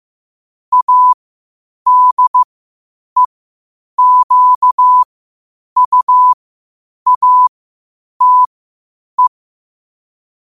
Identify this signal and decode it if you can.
Morse code, heard as a keyed tone, about 15 wpm, spaced as for 8 wpm, 1000 Hz. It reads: ADEQUATE
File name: qotdfiles / 8wpm